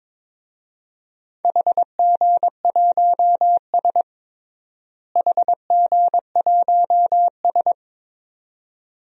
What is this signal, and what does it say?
Morse code, heard as a keyed tone, about 22 wpm, 700 Hz. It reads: HG1S HG1S
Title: DXCC-100 Call Signs - 4 Characters - Repeated 2x - 22wpm